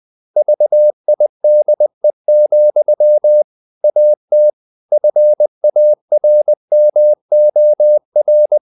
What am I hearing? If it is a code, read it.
Morse code, heard as a keyed tone, about 20 wpm, 600 Hz. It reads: VIDE, AT FARMOR
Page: Ferie 20 wpm